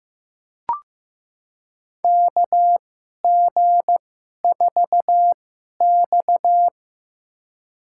Morse code, heard as a keyed tone, about 15 wpm, spaced as for 11 wpm, 700 Hz. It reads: KG4X
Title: USA Call Signs - 4 Characters - 2x Character Spacing - 15wpm